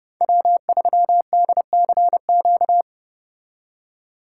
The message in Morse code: W3DCQ

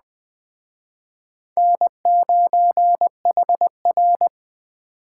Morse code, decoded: N9HR

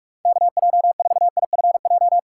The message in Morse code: KJ4IFJ

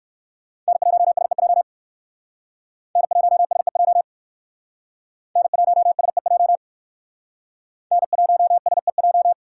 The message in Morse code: N1SEJ N1SEJ N1SEJ N1SEJ

